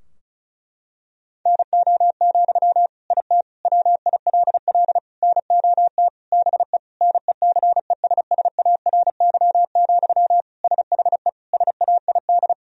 Morse code, heard as a keyed tone, about 34 wpm, 700 Hz. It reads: NO, IT WILL NOT BE NECESSARY, SHE SAID
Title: Sentences from Top 1000 Words - Review Part 4 - 35wpm